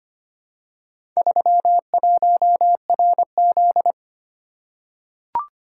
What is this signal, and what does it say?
Morse code, heard as a keyed tone, about 25 wpm, 700 Hz. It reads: 31RZ
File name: practice / 25wpm / Four Letter-Number Combinations - 25wpm